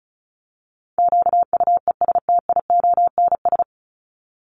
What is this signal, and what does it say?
Morse code, heard as a keyed tone, about 34 wpm, 700 Hz. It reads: QUESTIONS
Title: ES:S Suffix - Top 100 Words with Base Words 35wpm